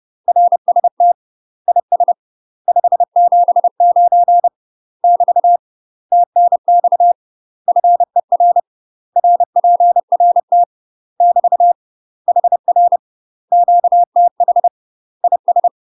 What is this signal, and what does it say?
Morse code, heard as a keyed tone, about 30 wpm, 700 Hz. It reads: RST IS 579 <BT> TNX FER RPRT <BT> HR QTH IS
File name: UK MASTER QSOs TRAINING/30WPM